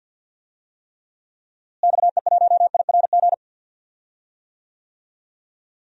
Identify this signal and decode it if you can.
Morse code, heard as a keyed tone, about 47 wpm, 700 Hz. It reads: XE1IRG
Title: DXCC-100 Call Signs - 6 Characters - Repeated 2x - 50wpm